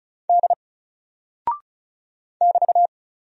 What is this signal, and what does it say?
Morse code, heard as a keyed tone, about 36 wpm, 700 Hz. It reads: D <BT>
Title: Single Character (Letter:Number:Punctuation:Prosign) Alt 35wpm